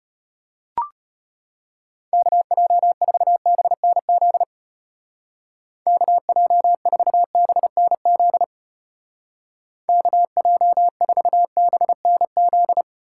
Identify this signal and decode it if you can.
Morse code, heard as a keyed tone, about 32 wpm, 700 Hz. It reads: KJ4BNZ KJ4BNZ KJ4BNZ